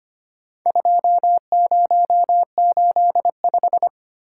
Morse code, heard as a keyed tone, about 25 wpm, 700 Hz. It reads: 2085
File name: practice / 25wpm / Four Letter-Number Combinations - 25wpm